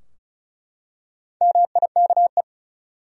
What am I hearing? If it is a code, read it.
Morse code, heard as a keyed tone, about 35 wpm, 700 Hz. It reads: MIKE
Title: USA Common Names - Max 5 Letters - 35wpm